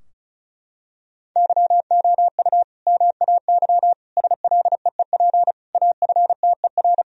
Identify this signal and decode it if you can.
Morse code, heard as a keyed tone, about 36 wpm, 700 Hz. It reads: YOU MAY SLEEP AFTER
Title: Sentences from Top 1000 Words - Review Part 4 - 35wpm